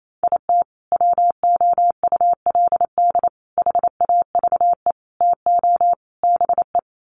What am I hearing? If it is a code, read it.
Morse code, heard as a keyed tone, about 28 wpm, 700 Hz. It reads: IT WOULD HAVE TO BE
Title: Sets of 5 Words Encore 28wpm